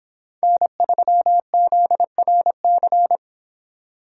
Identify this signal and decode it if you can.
Morse code, heard as a keyed tone, about 24 wpm, 700 Hz. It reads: N3ZRC